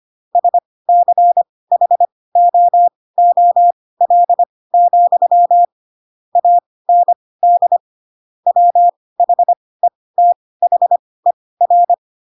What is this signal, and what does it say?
Morse code, heard as a keyed tone, about 25 wpm, spaced as for 18 wpm, 700 Hz. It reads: SCHOOL, AND WHETHER
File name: Pinocchio / Pinocchio ch4 12 25 ik1hge